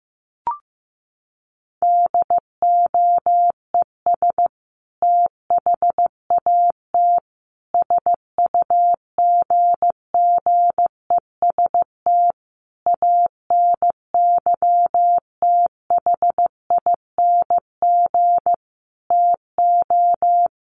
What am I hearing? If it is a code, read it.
Morse code, heard as a keyed tone, about 14 wpm, 700 Hz. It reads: DOES THAT SUGGEST ANYTHING TO
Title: Sentences from Top 1300 Words 15wpm